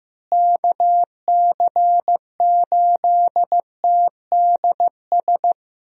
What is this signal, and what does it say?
Morse code, heard as a keyed tone, about 15 wpm, 700 Hz. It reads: KC8TDS
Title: USA Call Signs - 6 Characters - Repeated 3x - 15wpm